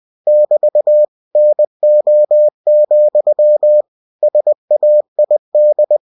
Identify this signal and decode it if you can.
Morse code, heard as a keyed tone, about 20 wpm, 600 Hz. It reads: <BT> NO, SAID
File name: cw-books / winnie-the-pooh / winnie-20wpm-page-0023